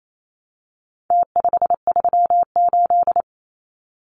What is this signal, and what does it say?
Morse code, heard as a keyed tone, about 28 wpm, 700 Hz. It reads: T538
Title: Numbers - Contest Style - 4-Digit 28wpm